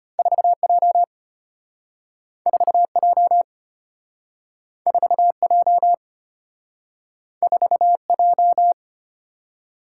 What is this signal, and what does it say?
Morse code, heard as a keyed tone, about 34 wpm, 690 Hz. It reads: 4J 4J 4J 4J